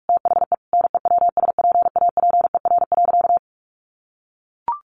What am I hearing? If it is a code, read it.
Morse code, heard as a keyed tone, about 44 wpm, 700 Hz. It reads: THE NEWSPAPER.
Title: Sentences from Top 1400 Words 45wpm